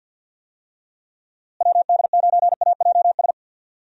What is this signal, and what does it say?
Morse code, heard as a keyed tone, about 47 wpm, 700 Hz. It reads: WD9AJS
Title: USA Call Signs - 6 Characters - Repeated 3x - 50wpm